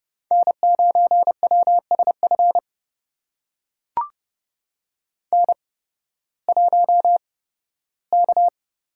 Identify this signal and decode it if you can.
Morse code, heard as a keyed tone, about 29 wpm, spaced as for 16 wpm, 700 Hz. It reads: N9WSF N1K
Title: USA Call Signs - 5 Characters - 8x Character Spacing - 30wpm